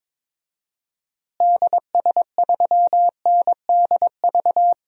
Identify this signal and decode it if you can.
Morse code, heard as a keyed tone, about 22 wpm, 700 Hz. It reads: DS3NDV
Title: DXCC-100 Call Signs - 6 Characters - Repeated 2x - 22wpm